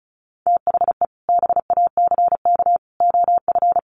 Morse code, heard as a keyed tone, about 34 wpm, 700 Hz. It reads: THE BACK OF
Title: Sets of 3 Words 35wpm